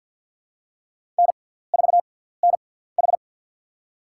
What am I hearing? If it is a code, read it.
Morse code, heard as a keyed tone, about 48 wpm, spaced as for 17 wpm, 700 Hz. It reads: N4NH
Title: USA Call Signs - 4 Characters - 6x Character Spacing - 50wpm